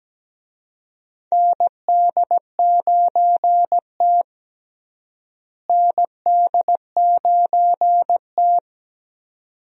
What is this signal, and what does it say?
Morse code, heard as a keyed tone, about 17 wpm, 700 Hz. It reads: ND9T ND9T
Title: USA Call Signs - 4 Characters - Repeated 2x - 17wpm